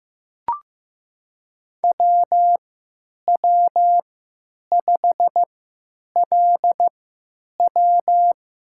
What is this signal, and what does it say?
Morse code, heard as a keyed tone, about 15 wpm, spaced as for 9 wpm, 700 Hz. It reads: WW5LW
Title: USA Call Signs - 3x Character Spacing - 15wpm